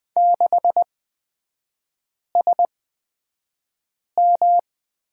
Morse code, heard as a keyed tone, about 20 wpm, spaced as for 5 wpm, 700 Hz. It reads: 6SM
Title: 255 Lesson 41e - Period - Three Characters 20wpm